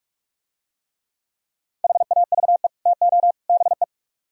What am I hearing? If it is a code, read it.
Morse code, heard as a keyed tone, about 44 wpm, 700 Hz. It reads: HAVE TO BE